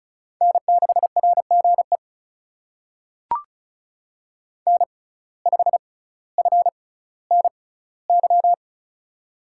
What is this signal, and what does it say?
Morse code, heard as a keyed tone, about 36 wpm, spaced as for 18 wpm, 700 Hz. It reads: N6RGE N5FNY
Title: USA Call Signs - 5 Characters - 6x Character Spacing - 35wpm